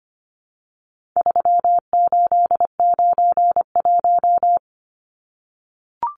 Morse code, heard as a keyed tone, about 25 wpm, 700 Hz. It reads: 3891
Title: Numbers - 4-Digit 25wpm